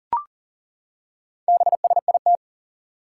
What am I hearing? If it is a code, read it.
Morse code, heard as a keyed tone, about 38 wpm, 700 Hz. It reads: BSIT
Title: Four Challenging Letter-Number Combinations 40wpm